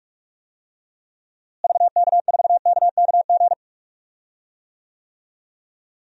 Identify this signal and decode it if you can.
Morse code, heard as a keyed tone, about 42 wpm, 700 Hz. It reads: VK4KKG
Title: DXCC-100 Call Signs - 6 Characters - Repeated 2x - 45wpm